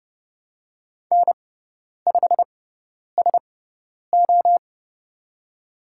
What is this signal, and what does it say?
Morse code, heard as a keyed tone, about 30 wpm, spaced as for 10 wpm, 700 Hz. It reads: N5SO
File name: Lesson 17g - Letter u - Call Signs 30wpm